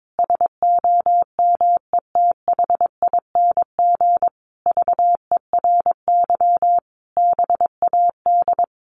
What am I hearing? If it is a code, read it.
Morse code, heard as a keyed tone, about 22 wpm, 700 Hz. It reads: SOMETHING VERY BAD
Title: Sentences from Top 700 Words 22wpm